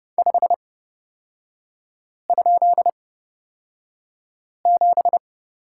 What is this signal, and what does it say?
Morse code, heard as a keyed tone, about 30 wpm, spaced as for 5 wpm, 700 Hz. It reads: 5?7
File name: Lesson 36e - BT - Three Characters 30wpm